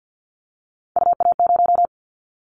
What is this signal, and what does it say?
Morse code, heard as a keyed tone, about 50 wpm, 700 Hz. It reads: UA0
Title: Callsign Prefixes for Top 25 DXCC - 50wpm